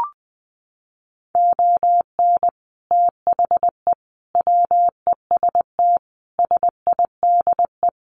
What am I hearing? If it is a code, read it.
Morse code, heard as a keyed tone, about 20 wpm, 700 Hz. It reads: ON THE WEST SIDE
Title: Sets of 5 Words Encore2 20wpm